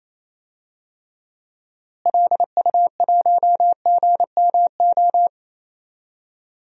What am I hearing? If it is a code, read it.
Morse code, heard as a keyed tone, about 28 wpm, 700 Hz. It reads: LU1GMO
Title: DXCC-100 Call Signs - 6 Characters - Repeated 2x - 28wpm